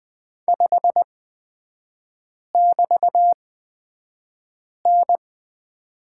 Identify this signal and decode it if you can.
Morse code, heard as a keyed tone, about 20 wpm, spaced as for 5 wpm, 700 Hz. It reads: 5<BT>N
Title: 224 Lesson 36e - BT - Three Characters 20wpm